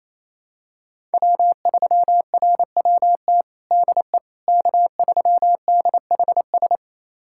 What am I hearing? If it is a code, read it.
Morse code, heard as a keyed tone, about 28 wpm, 700 Hz. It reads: W3RWT DE K3DHS
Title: USA Callsign DE Callsign - 5 Characters - 28wpm